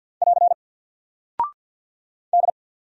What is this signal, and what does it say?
Morse code, heard as a keyed tone, about 50 wpm, 700 Hz. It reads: <AR> D
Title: Single Character (Letter:Number:Punctuation:Prosign) Alt 50wpm